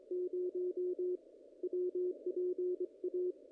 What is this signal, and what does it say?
Morse code, heard as a keyed tone, about 26 wpm, 410 Hz. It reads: 0 WPA
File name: contests / ss-cw-k5tr-op-n5rz -0100